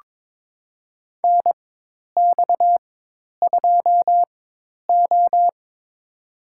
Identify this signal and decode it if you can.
Morse code, heard as a keyed tone, about 22 wpm, spaced as for 10 wpm, 700 Hz. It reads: NX2O
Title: USA Call Signs - 4 Characters - 4x Character Spacing - 22wpm